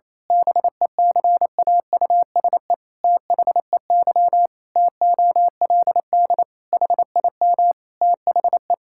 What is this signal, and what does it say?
Morse code, heard as a keyed tone, about 28 wpm, 700 Hz. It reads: BECAUSE THEY TOLD HIM THE
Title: Pangram Challenge v2 - 28wpm